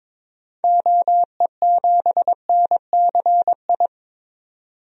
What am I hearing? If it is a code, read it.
Morse code, heard as a keyed tone, about 22 wpm, 700 Hz. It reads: OE7NCI
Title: DXCC-100 Call Signs - 6 Characters - 22wpm